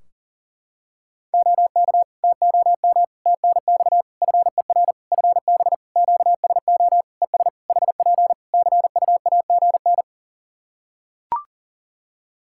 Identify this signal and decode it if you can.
Morse code, heard as a keyed tone, about 40 wpm, 700 Hz. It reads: OK TOM TNX FER FB QSO ES HP CUAGN
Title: Practice QSO - Exchange 4 - Ending - 40wpm